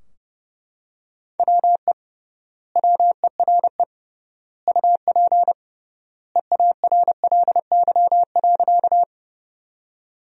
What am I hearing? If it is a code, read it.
Morse code, heard as a keyed tone, about 30 wpm, 700 Hz. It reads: WE WERE UP EARLY.
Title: 3x Word-Spacing - Sentences from Top 400 Words - 30wpm